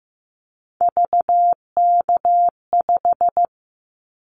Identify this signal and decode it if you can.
Morse code, heard as a keyed tone, about 15 wpm, 700 Hz. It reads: VK5
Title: Callsign Prefixes for Top 50 DXCC - 15wpm